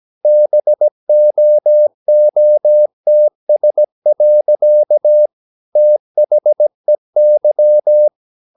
Morse code, heard as a keyed tone, about 17 wpm, 600 Hz. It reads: BOOTS. THEY
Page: Romanen Never af Ken Follet på engelsk.